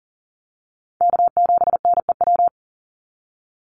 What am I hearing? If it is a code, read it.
Morse code, heard as a keyed tone, about 40 wpm, 700 Hz. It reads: K7NEW
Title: NA SOTA Chaser Callsigns 40wpm